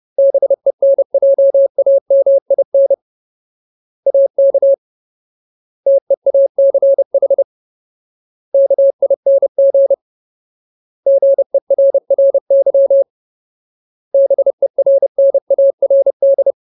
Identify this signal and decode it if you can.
Morse code, heard as a keyed tone, about 30 wpm, 550 Hz. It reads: BENJAMIN AK TEACH KING GERRY BERNARD
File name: Ultimate Word List / 30WPM Ultimate